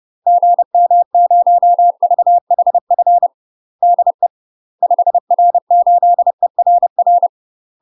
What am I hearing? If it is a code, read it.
Morse code, heard as a keyed tone, about 30 wpm, 700 Hz. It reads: GM0VHF DE 5R8ERR
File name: UK MASTER QSOs TRAINING / 30WPM / UK MASTER QSOs 0043